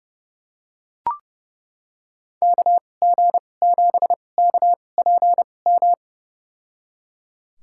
Morse code, heard as a keyed tone, about 30 wpm, spaced as for 22 wpm, 700 Hz. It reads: KG7KPM
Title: USA Call Signs - 6 Characters - 2x Character Spacing - 30wpm